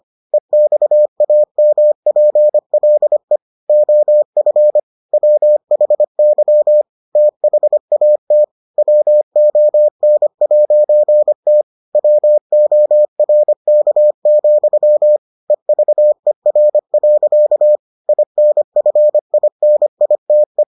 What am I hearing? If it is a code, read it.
Morse code, heard as a keyed tone, about 25 wpm, 600 Hz. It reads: EXAMPLE OF WHY THAT WON'T WORK, EVER. INFINITE